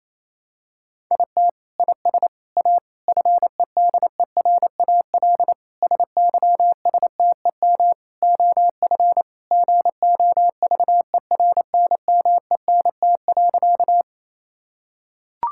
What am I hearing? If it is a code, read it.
Morse code, heard as a keyed tone, about 28 wpm, 700 Hz. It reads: IT IS A FEDERAL SYSTEM OF GOVERNMENT.
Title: Sentences from Top 1000 Words - Review Part 5 - 28wpm